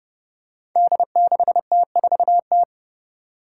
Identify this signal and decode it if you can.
Morse code, heard as a keyed tone, about 29 wpm, 700 Hz. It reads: D6T4T
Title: Five Challenging Letter-Number Combinations 30wpm